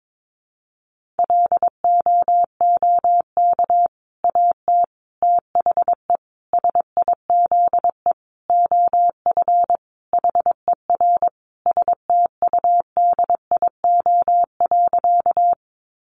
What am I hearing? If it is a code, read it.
Morse code, heard as a keyed tone, about 22 wpm, 700 Hz. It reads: LOOK AT THE SIZE OF HER STUDIO.
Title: Sentences from Top 1200 Words 22wpm